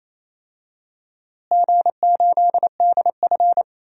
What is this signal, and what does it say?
Morse code, heard as a keyed tone, about 28 wpm, 700 Hz. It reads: G8DF
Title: DXCC-100 Call Signs - 4 Characters - Repeated 3x - 28wpm